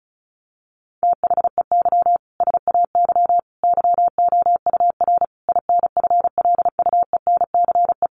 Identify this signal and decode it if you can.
Morse code, heard as a keyed tone, about 34 wpm, 700 Hz. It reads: THEY SAY YOUR INFLUENCE
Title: Sentences from Top 1100 Words 35wpm